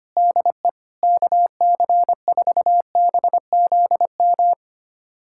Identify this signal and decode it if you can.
Morse code, heard as a keyed tone, about 25 wpm, 700 Hz. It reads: DE KC4BZM
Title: USA Callsign DE Callsign - 6 Characters - 25wpm